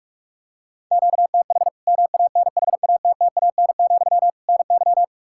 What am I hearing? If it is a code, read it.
Morse code, heard as a keyed tone, about 46 wpm, 700 Hz. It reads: QTH MANHATTAN, NY
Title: Practice QSO - Exchange 1 - Beginning - 45wpm